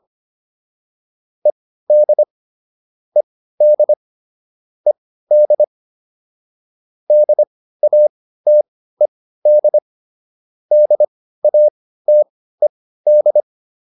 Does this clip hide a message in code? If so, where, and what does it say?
Yes: Morse code from 1.5 to 13.4 s, heard as a keyed tone, about 25 wpm, spaced as for 15 wpm, 600 Hz. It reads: ED ED ED DATED DATED